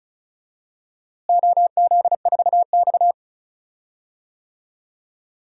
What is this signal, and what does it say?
Morse code, heard as a keyed tone, about 34 wpm, 700 Hz. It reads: OZ4X
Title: DXCC-100 Call Signs - 4 Characters - Repeated 3x - 35wpm